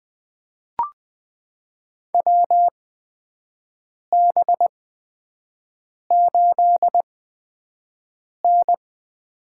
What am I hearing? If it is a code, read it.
Morse code, heard as a keyed tone, about 20 wpm, spaced as for 5 wpm, 700 Hz. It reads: WB8N
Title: USA Call Signs - 6 Characters - 8x Character Spacing - 20wpm